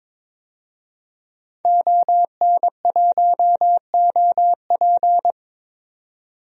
Morse code, heard as a keyed tone, about 22 wpm, 700 Hz. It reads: ON1OP
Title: DXCC-100 Call Signs - 5 Characters - 22wpm